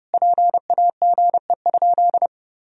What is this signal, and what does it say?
Morse code, heard as a keyed tone, about 30 wpm, 700 Hz. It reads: PAGE?
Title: Sentences from Top 1000 Words - Review Part 1 - 30wpm